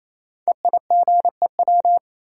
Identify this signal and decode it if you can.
Morse code, heard as a keyed tone, about 29 wpm, 700 Hz. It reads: EIGEW
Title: Code Groups - 5 Letters - 28wpm